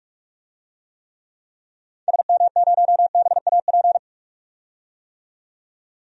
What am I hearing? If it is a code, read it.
Morse code, heard as a keyed tone, about 46 wpm, 700 Hz. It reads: SM0BAP
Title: DXCC-100 Call Signs - 6 Characters - 45wpm